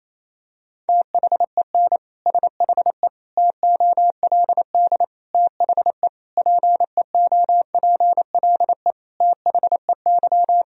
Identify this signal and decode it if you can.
Morse code, heard as a keyed tone, about 28 wpm, 700 Hz. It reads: THEN SHE TOLD THE PEOPLE THEY
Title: Sentences from Top 1000 Words - Review Part 2 - 28wpm